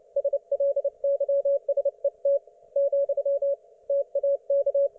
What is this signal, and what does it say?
Morse code, heard as a keyed tone, about 29 wpm, 560 Hz. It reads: SLYSET , TAK